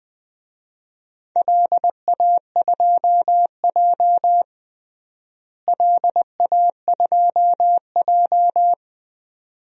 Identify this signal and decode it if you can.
Morse code, heard as a keyed tone, about 20 wpm, 700 Hz. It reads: LA2J LA2J
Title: DXCC-100 Call Signs - 4 Characters - Repeated 2x - 20wpm